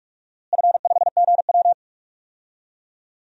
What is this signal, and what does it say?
Morse code, heard as a keyed tone, about 46 wpm, 700 Hz. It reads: F5GW
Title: DXCC-100 Call Signs - 4 Characters - Repeated 3x - 45wpm